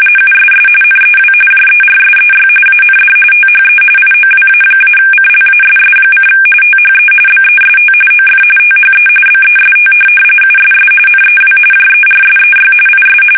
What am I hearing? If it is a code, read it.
RTTY, about 45.45 baud. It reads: USS NEW JERSEY AND THE IOWA-CLASS: THE BEST OF THE BEST USS NEW JERSEY IS
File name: fsk850 / battleshipnj